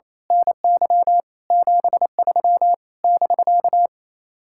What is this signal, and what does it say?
Morse code, heard as a keyed tone, about 26 wpm, 700 Hz. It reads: NY 73 <BK>
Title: Speed-Racing Common QSO Phrases - 28wpm